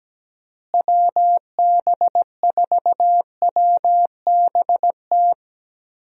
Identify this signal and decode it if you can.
Morse code, heard as a keyed tone, about 17 wpm, 700 Hz. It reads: WB4WBT
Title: USA Call Signs - 6 Characters - Repeated 3x - 17wpm